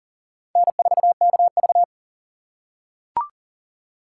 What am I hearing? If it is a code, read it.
Morse code, heard as a keyed tone, about 40 wpm, 700 Hz. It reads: N4KV
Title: USA Call Signs - 4 Characters - 3x Character Spacing - 40wpm